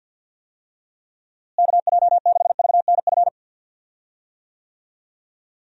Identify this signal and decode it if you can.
Morse code, heard as a keyed tone, about 52 wpm, 700 Hz. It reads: KJ6VNF
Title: USA Call Signs - 6 Characters - Repeated 3x - 50wpm